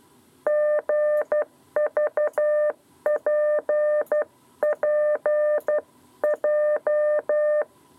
Morse code, heard as a keyed tone, about 12 wpm, 600 Hz. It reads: GVPPJ
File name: 7b lezione cw 2